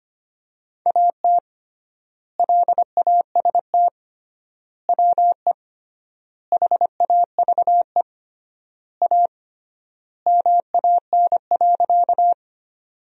Morse code, heard as a keyed tone, about 25 wpm, 700 Hz. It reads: AT LAST WE HAVE A MAN.
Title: 3x Word-Spacing - Sentences from Top 300 Words - 25wpm